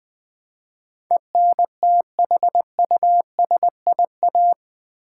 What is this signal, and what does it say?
Morse code, heard as a keyed tone, about 20 wpm, 700 Hz. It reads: ENTHUSIA
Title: Build A Word - 4 Letters At A Time - 10-Letter Words - 20wpm